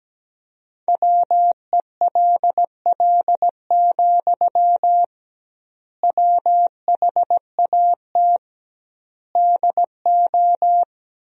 Morse code, heard as a keyed tone, about 17 wpm, 700 Hz. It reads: WELL, WHAT DO
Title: 2x Word-Spacing - Sentences from Top 200 Words - 17wpm